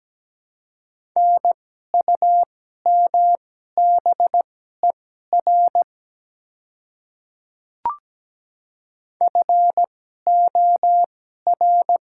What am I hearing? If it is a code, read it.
Morse code, heard as a keyed tone, about 17 wpm, spaced as for 12 wpm, 700 Hz. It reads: NUMBER FOR
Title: 2x Character Spacing - Top 100 Words - 17wpm